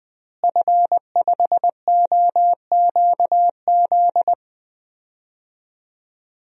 Morse code, heard as a keyed tone, about 20 wpm, 700 Hz. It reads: F5OQZ
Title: DXCC-100 Call Signs - 5 Characters - Repeated 2x - 20wpm